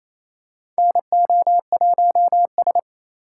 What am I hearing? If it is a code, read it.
Morse code, heard as a keyed tone, about 27 wpm, 700 Hz. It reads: NO1S
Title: USA Call Signs - 4 Characters - 2x Character Spacing - 28wpm